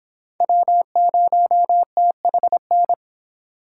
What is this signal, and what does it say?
Morse code, heard as a keyed tone, about 31 wpm, 700 Hz. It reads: W0THN